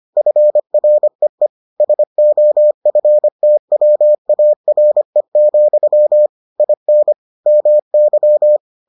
Morse code, heard as a keyed tone, about 25 wpm, 600 Hz. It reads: FREE SOFTWARE, IN MY